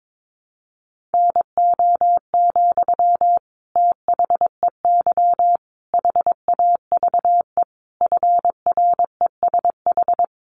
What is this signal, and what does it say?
Morse code, heard as a keyed tone, about 22 wpm, 700 Hz. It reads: NO, THEY HAVE FRESH
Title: Sentences from Top 1300 Words 22wpm